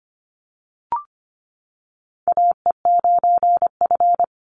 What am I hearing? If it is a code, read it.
Morse code, heard as a keyed tone, about 25 wpm, 700 Hz. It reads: AE9F
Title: NA SOTA Chaser Callsigns 25wpm